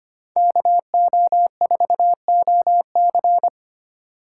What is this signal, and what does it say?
Morse code, heard as a keyed tone, about 25 wpm, 700 Hz. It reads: KO4OC
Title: USA Call Signs - 5 Characters - 6x Character Spacing - 25wpm